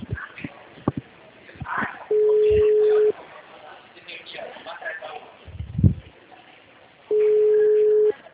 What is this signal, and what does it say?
Ringback tone (Europe, 425 Hz), with speech.
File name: out- -2003- - - .78283